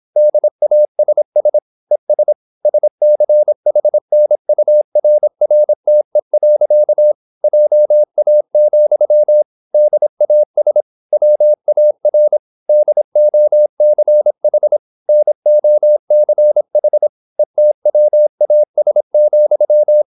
Morse code, heard as a keyed tone, about 26 wpm, 600 Hz. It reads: DASS ES SCHNURRTE. JA, DAS WAR DOCH NOCH ETWAS,